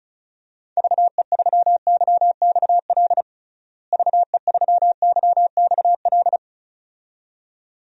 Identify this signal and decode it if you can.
Morse code, heard as a keyed tone, about 36 wpm, 700 Hz. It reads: VE3YXL VE3YXL
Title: DXCC-100 Call Signs - 6 Characters - Repeated 2x - 35wpm